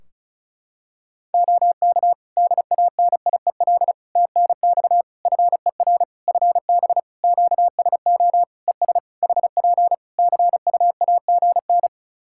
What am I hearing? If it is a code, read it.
Morse code, heard as a keyed tone, about 34 wpm, 700 Hz. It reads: OK DANIEL TNX FER FB QSO ES HP CUAGN
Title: Practice QSO - Exchange 4 - Ending - 35wpm